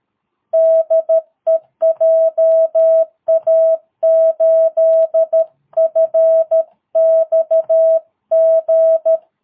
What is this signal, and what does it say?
Morse code, heard as a keyed tone, about 13 wpm, 660 Hz. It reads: DE JA8FXG
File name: jh8cbh / 2022cw